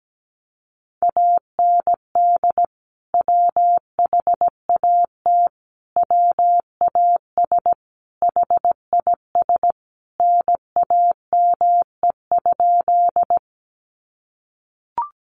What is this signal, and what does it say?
Morse code, heard as a keyed tone, about 17 wpm, 700 Hz. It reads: AND WHAT WAS HIS NAME?
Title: Sentences from Top 200 Words 17wpm